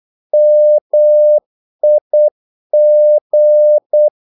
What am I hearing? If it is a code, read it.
Morse code, heard as a keyed tone, about 8 wpm, 600 Hz. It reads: MIG
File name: Korte ord DK 8wpm